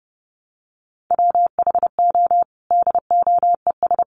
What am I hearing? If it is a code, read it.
Morse code, heard as a keyed tone, about 30 wpm, 700 Hz. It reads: WHO DOES
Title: Sentences from Top 1300 Words 30wpm